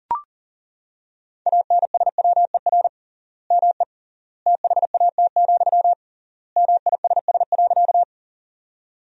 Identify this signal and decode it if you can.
Morse code, heard as a keyed tone, about 40 wpm, 700 Hz. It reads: ANSWER ME THAT, MISS.
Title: 3x Word-Spacing - Sentences from Top 500 Words - 40wpm